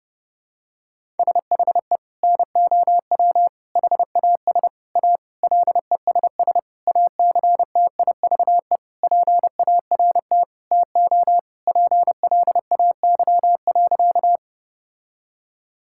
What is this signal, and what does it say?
Morse code, heard as a keyed tone, about 30 wpm, 700 Hz. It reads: SHE NOW HAS A LESS ACTIVE PART TO PLAY.
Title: Sentences from Top 1000 Words - Review Part 6 - 30wpm